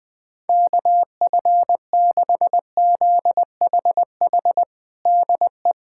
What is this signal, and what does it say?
Morse code, heard as a keyed tone, about 20 wpm, 700 Hz. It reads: KF6ZHH DE
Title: USA Callsign DE Callsign 20wpm